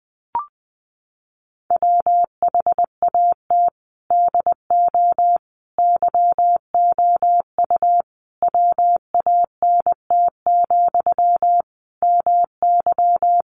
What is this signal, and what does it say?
Morse code, heard as a keyed tone, about 20 wpm, 700 Hz. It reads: WHAT DO YOU WANT, MY
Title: Sentences from Top 1000 Words 20wpm